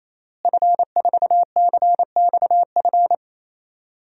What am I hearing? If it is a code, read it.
Morse code, heard as a keyed tone, about 28 wpm, 700 Hz. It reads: F4CXF
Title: DXCC-100 Call Signs - 5 Characters - Repeated 3x - 28wpm